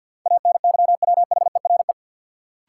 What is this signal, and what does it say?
Morse code, heard as a keyed tone, about 52 wpm, 700 Hz. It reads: ANYWHERE
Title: Sentences from Top 1400 Words - New Word Drill Alt - 50wpm